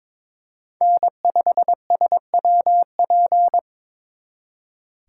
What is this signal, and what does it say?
Morse code, heard as a keyed tone, about 25 wpm, 700 Hz. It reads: N5SWP
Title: Speed Racing - USA 1x3 Call Signs 22wpm